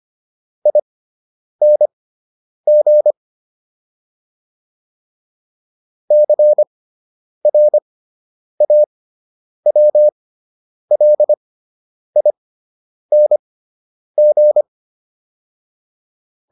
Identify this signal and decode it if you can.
Morse code, heard as a keyed tone, about 25 wpm, spaced as for 9 wpm, 600 Hz. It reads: ING CRAWLING